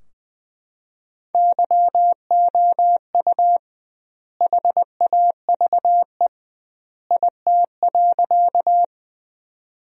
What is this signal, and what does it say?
Morse code, heard as a keyed tone, about 20 wpm, 700 Hz. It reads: YOU HAVE IT.
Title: 2x Word-Spacing - Sentences from Top 100 Words - 20wpm